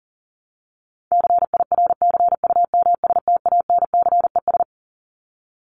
Morse code, heard as a keyed tone, about 40 wpm, 700 Hz. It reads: CIRCUMSTANCES
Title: ES:S Suffix - Top 100 Words with Base Words 40wpm